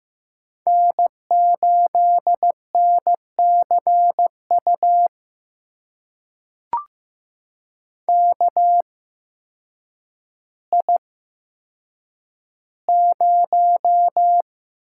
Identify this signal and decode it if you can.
Morse code, heard as a keyed tone, about 15 wpm, spaced as for 8 wpm, 700 Hz. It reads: N8NCU KI0
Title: USA Call Signs - 5 Characters - 8x Character Spacing - 15wpm